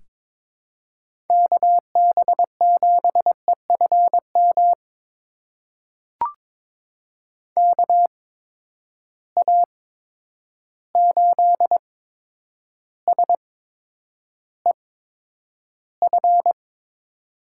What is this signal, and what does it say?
Morse code, heard as a keyed tone, about 22 wpm, spaced as for 9 wpm, 700 Hz. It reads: KB7EFM KA8SEF
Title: USA Call Signs - 6 Characters - 8x Character Spacing - 22wpm